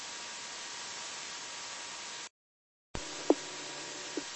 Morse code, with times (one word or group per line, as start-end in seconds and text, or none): none